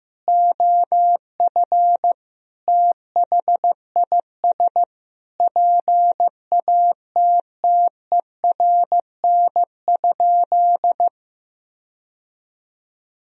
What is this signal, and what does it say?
Morse code, heard as a keyed tone, about 15 wpm, 700 Hz. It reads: OF THIS PATTERN?
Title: Sentences from Top 1000 Words - Review Part 2 - 15wpm